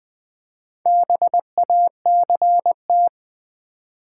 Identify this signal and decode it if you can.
Morse code, heard as a keyed tone, about 20 wpm, 700 Hz. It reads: BACT
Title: Build A Word - 2 Letters At A Time - 8-Letter Words - 20wpm